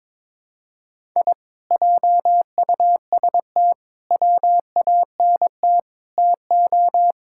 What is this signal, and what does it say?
Morse code, heard as a keyed tone, about 22 wpm, 700 Hz. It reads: I JUST WANT TO
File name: practice / 22wpm / Sentences with 2-Word Phrases - 22wpm